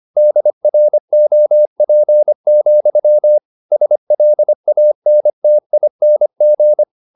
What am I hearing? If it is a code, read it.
Morse code, heard as a keyed tone, about 25 wpm, 600 Hz. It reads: DROP, SLANTING
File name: the old man and the sea part 1 25wpm